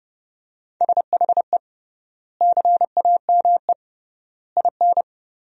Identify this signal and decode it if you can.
Morse code, heard as a keyed tone, about 32 wpm, 700 Hz. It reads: SHE CAME IN
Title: 3x Word-Spacing - Sentences from Top 300 Words - 30wpm